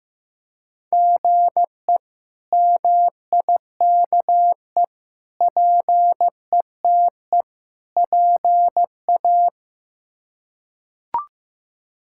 Morse code, heard as a keyed tone, about 15 wpm, 700 Hz. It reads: GE MIKE PETE PA
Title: K1USN SST QSO Practice - 15wpm